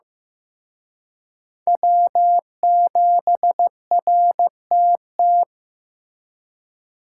Morse code, heard as a keyed tone, about 18 wpm, 700 Hz. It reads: W7RTT
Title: Speed Racing - USA 1x3 Call Signs 15wpm